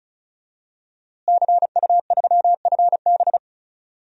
Morse code, heard as a keyed tone, about 36 wpm, 700 Hz. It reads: CU3FB
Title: DXCC-100 Call Signs - 5 Characters - Repeated 2x - 35wpm